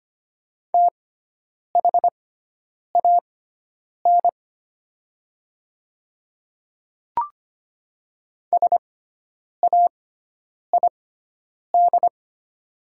Morse code, heard as a keyed tone, about 25 wpm, spaced as for 9 wpm, 700 Hz. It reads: THAN SAID
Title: 6x Character Spacing - Top 100 Words - 25wpm